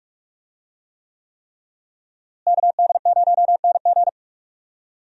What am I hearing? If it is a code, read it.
Morse code, heard as a keyed tone, about 44 wpm, 700 Hz. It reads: KD0NG
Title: USA Call Signs - 5 Characters - Repeated 3x - 45wpm